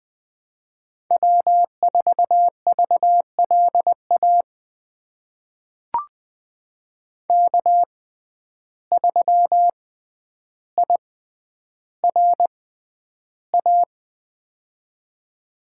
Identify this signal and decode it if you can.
Morse code, heard as a keyed tone, about 20 wpm, spaced as for 10 wpm, 700 Hz. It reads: W4VLA K3IRA
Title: USA Call Signs - 5 Characters - 6x Character Spacing - 20wpm